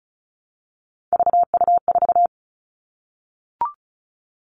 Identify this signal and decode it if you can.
Morse code, heard as a keyed tone, about 34 wpm, 700 Hz. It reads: VU4
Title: All Cut Numbers - 3-Digit 35wpm